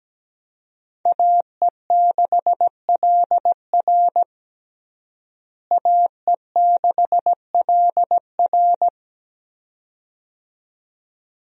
Morse code, heard as a keyed tone, about 17 wpm, 700 Hz. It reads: AE6LR AE6LR
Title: USA Call Signs - 5 Characters - Repeated 2x - 17wpm